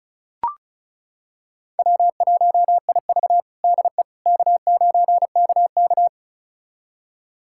Morse code, heard as a keyed tone, about 34 wpm, 700 Hz. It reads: W1IV DE K9KK
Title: USA Callsign DE Callsign - 4 Characters - 35wpm